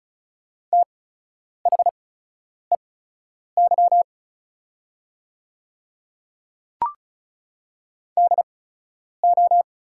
Morse code, heard as a keyed tone, about 36 wpm, spaced as for 10 wpm, 700 Hz. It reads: THEY DO
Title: 8x Character Spacing - Top 100 Words - 35wpm